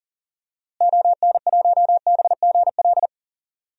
Morse code, heard as a keyed tone, about 40 wpm, 700 Hz. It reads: ON1BGL
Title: DXCC-100 Call Signs - 6 Characters - Repeated 2x - 40wpm